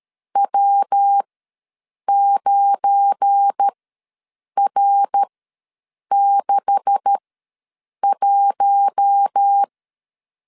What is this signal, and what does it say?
Morse code, heard as a keyed tone, about 13 wpm, spaced as for 7 wpm, 800 Hz. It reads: W9R61
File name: LEZIONI / 9 286 ws 12wpm 3351